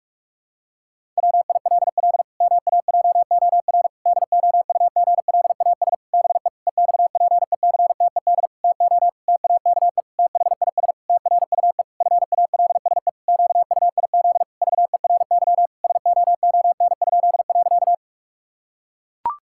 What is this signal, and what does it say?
Morse code, heard as a keyed tone, about 41 wpm, 700 Hz. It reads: WILL MAJOR DOUGLAS BE EXPECTED TO TAKE THIS TRUE FALSE QUIZ VERY SOON?.
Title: Pangram Challenge v2 - 45wpm